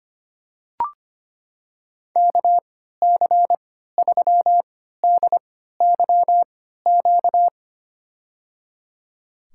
Morse code, heard as a keyed tone, about 25 wpm, spaced as for 14 wpm, 700 Hz. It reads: KC3DYQ
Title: USA Call Signs - 6 Characters - 3x Character Spacing - 25wpm